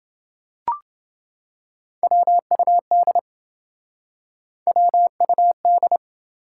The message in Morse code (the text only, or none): WUD WUD